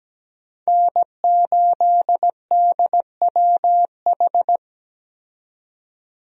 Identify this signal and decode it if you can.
Morse code, heard as a keyed tone, about 17 wpm, 700 Hz. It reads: N8DWH
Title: USA Call Signs - 5 Characters - 17wpm